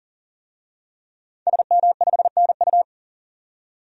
Morse code, heard as a keyed tone, about 40 wpm, 700 Hz. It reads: SM5NU
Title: DXCC-100 Call Signs - 5 Characters - 40wpm